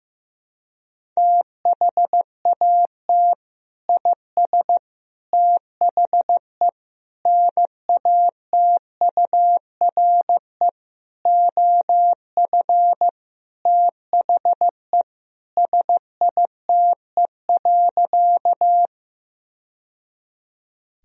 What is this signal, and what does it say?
Morse code, heard as a keyed tone, about 15 wpm, 700 Hz. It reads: THAT IS THE NATURE OF THE SITE.
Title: Sentences from Top 1000 Words - Review Part 4 - 15wpm